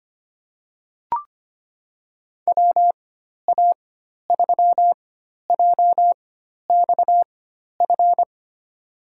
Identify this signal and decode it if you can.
Morse code, heard as a keyed tone, about 25 wpm, spaced as for 12 wpm, 700 Hz. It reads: WA3JXF
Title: USA Call Signs - 6 Characters - 4x Character Spacing - 25wpm